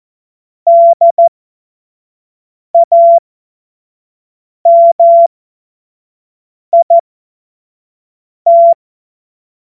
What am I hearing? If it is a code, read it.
Morse code, heard as a keyed tone, about 14 wpm, spaced as for 5 wpm, 680 Hz. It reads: DAMIT
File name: Morsekurs / Witze / Witz 22. Juni 2021, 25 F70